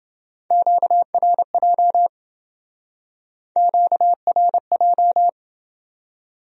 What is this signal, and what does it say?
Morse code, heard as a keyed tone, about 28 wpm, 700 Hz. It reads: QRJ QRJ